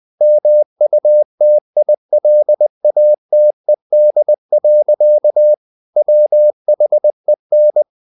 Morse code, heard as a keyed tone, about 20 wpm, 600 Hz. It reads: MUTILATED. WHEN
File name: the old man and the sea part 2 20wpm